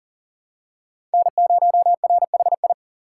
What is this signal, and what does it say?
Morse code, heard as a keyed tone, about 37 wpm, 700 Hz. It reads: N0RHI
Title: USA Call Signs - 5 Characters - 2x Character Spacing - 40wpm